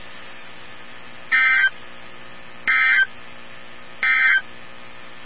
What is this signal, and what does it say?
Emergency Alert System (SAME) tones, heard as harsh buzzing bursts. It is heard from 1.2 to 4.5 s.
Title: NOAA Weather Radio 162.550